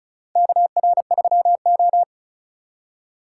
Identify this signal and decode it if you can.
Morse code, heard as a keyed tone, about 36 wpm, 700 Hz. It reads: KR3O
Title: USA Call Signs - 4 Characters - 3x Character Spacing - 35wpm